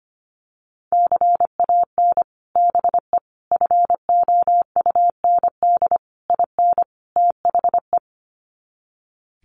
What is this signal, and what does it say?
Morse code, heard as a keyed tone, about 25 wpm, 700 Hz. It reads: CAN BE FOUND IN THE
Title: Sets of 5 Words Encore 25wpm